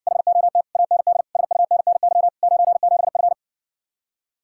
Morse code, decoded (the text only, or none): HOT AND SUNNY 97F